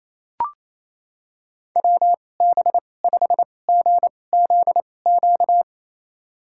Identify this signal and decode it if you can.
Morse code, heard as a keyed tone, about 28 wpm, spaced as for 20 wpm, 700 Hz. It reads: WB5GZQ
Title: USA Call Signs - 6 Characters - 2x Character Spacing - 28wpm